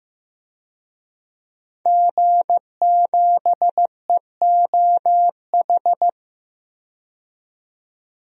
Morse code, heard as a keyed tone, about 15 wpm, 700 Hz. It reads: G7EOH
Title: DXCC-100 Call Signs - 5 Characters - 15wpm